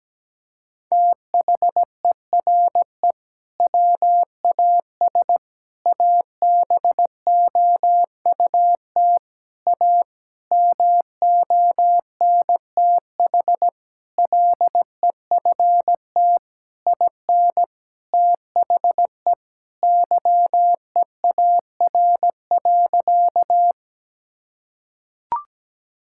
Morse code, heard as a keyed tone, about 17 wpm, 700 Hz. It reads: THERE WAS ABOUT A MONTH LEFT IN THE YEAR.
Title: Sentences from Top 1000 Words - Review Part 5 - 17wpm